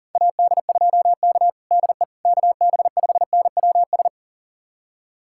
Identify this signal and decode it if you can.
Morse code, heard as a keyed tone, about 40 wpm, 700 Hz. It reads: AD2K DE KB5NWS
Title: USA Callsign DE Callsign 40wpm